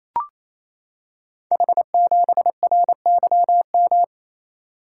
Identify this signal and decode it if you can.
Morse code, heard as a keyed tone, about 27 wpm, 700 Hz. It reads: H7RYM
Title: Code Groups - 5 Letter:Numbers - 28wpm